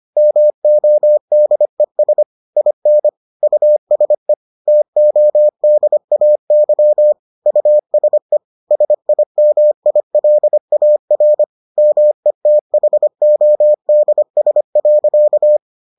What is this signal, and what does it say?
Morse code, heard as a keyed tone, about 25 wpm, 600 Hz. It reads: MODES IN USE TODAY USE SIMILAR METHODS.